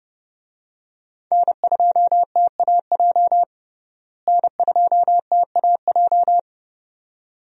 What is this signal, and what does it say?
Morse code, heard as a keyed tone, about 30 wpm, 700 Hz. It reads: N2TAJ N2TAJ
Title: USA Call Signs - 5 Characters - Repeated 2x - 30wpm